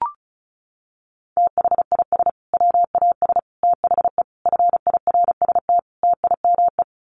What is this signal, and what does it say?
Morse code, heard as a keyed tone, about 34 wpm, 700 Hz. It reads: THIS WAS THE FIRST TIME
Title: Sets of 5 Words Encore 35wpm